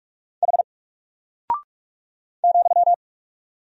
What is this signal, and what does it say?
Morse code, heard as a keyed tone, about 44 wpm, 700 Hz. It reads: H ,
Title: Single Character (Letter:Number:Punctuation:Prosign) Alt 45wpm